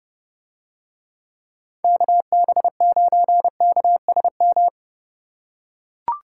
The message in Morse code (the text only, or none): KB9KSM